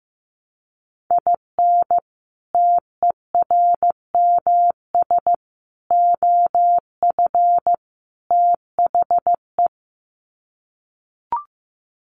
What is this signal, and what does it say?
Morse code, heard as a keyed tone, about 15 wpm, 700 Hz. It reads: IN TERMS OF THE
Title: Sets of 4 Words 15wpm